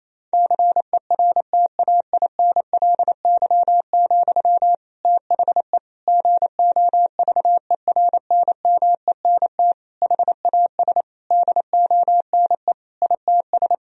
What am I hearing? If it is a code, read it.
Morse code, heard as a keyed tone, about 28 wpm, 700 Hz. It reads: CERTAINLY, THE GOVERNMENT HAS DONE ITS
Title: Sentences from Top 1000 Words - Review Part 3 - 28wpm